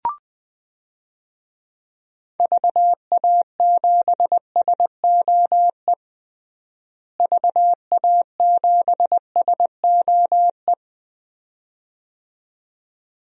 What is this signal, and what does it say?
Morse code, heard as a keyed tone, about 20 wpm, 700 Hz. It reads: VA7SOE VA7SOE
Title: DXCC-100 Call Signs - 6 Characters - Repeated 2x - 20wpm